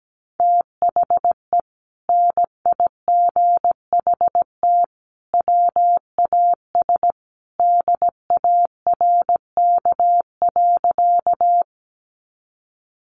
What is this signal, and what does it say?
Morse code, heard as a keyed tone, about 17 wpm, 700 Hz. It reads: THE NIGHT WAS DARK.
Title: Sentences from Top 500 Words 17wpm